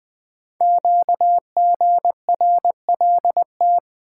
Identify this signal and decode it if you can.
Morse code, heard as a keyed tone, about 20 wpm, 700 Hz. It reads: QGRLT
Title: Code Groups - 5 Letters - 20wpm